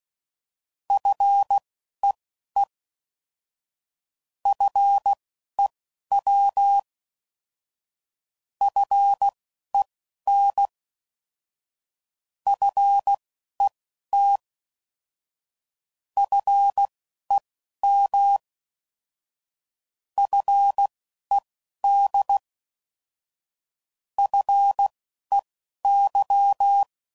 Morse code, heard as a keyed tone, about 16 wpm, spaced as for 11 wpm, 800 Hz. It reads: FEE FEW FEN FET FEM FED FEY
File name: Prefixes / By First Letter / Fs